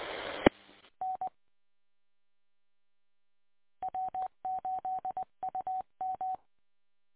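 Morse code, heard as a keyed tone, about 20 wpm, 750 Hz. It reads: E N W8UM
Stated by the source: Recorded from W8UM on 145.230 MHz.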